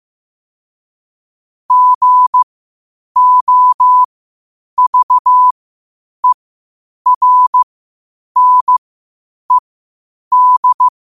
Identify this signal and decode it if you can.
Morse code, heard as a keyed tone, about 15 wpm, spaced as for 8 wpm, 1000 Hz. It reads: GOVERNED
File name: qotdfiles / 8wpm